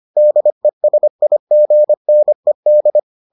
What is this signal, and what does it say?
Morse code, heard as a keyed tone, about 25 wpm, 600 Hz. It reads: DESIGNED